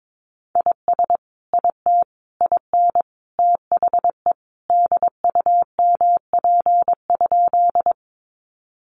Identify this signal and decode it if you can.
Morse code, heard as a keyed tone, about 22 wpm, 700 Hz. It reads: IS IT IN THE DUMP?
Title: Sentences from Top 1300 Words 22wpm